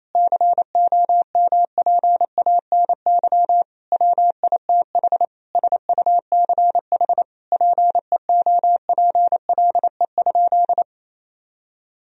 Morse code, heard as a keyed tone, about 28 wpm, 700 Hz. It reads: COMPANY WITH SUCH PEOPLE?
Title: Sentences from Top 1000 Words - Review Part 3 - 28wpm